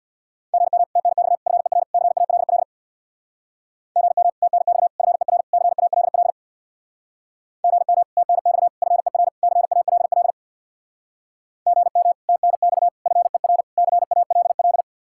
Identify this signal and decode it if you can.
Morse code, heard as a keyed tone, about 67 wpm, 700 Hz. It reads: GM TNX FER CALL GM TNX FER CALL GM TNX FER CALL GM TNX FER CALL